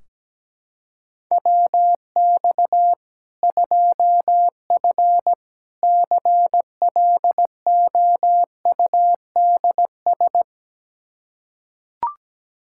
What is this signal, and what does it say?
Morse code, heard as a keyed tone, about 17 wpm, 700 Hz. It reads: WX 2F CLOUDS
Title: Practice QSO - Exchange 2 - Info - 17wpm